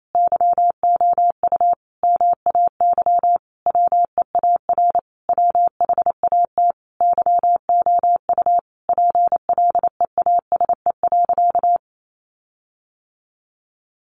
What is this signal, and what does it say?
Morse code, heard as a keyed tone, about 28 wpm, 700 Hz. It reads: YOU MAY WEAR WHAT YOU PLEASE.
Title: Sentences from Top 1500 Words 28wpm